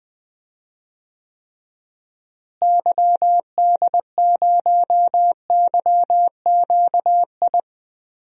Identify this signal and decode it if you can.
Morse code, heard as a keyed tone, about 20 wpm, 700 Hz. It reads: YD0YQI
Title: DXCC-100 Call Signs - 6 Characters - Repeated 2x - 20wpm